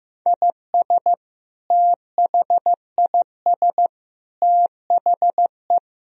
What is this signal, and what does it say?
Morse code, heard as a keyed tone, about 16 wpm, 700 Hz. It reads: IS THIS THE
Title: Sentences from Top 1000 Words - Review Part 4 - 15wpm